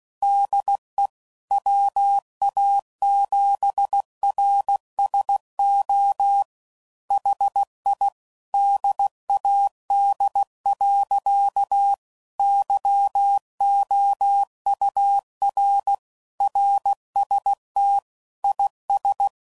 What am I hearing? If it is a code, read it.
Morse code, heard as a keyed tone, about 16 wpm, 800 Hz. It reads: DE WA7RSO HI DAD. YOUR RST IS
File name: CW Code Groups @15WPM